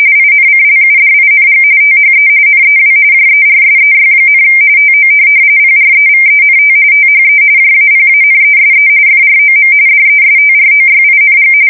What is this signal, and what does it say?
RTTY, about 45.45 baud. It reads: INCLUDING THE NEW JERSEY, TO PREVENT THE JAPANESE FROM DESTROYING
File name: fsk170 / USS-New-Jersey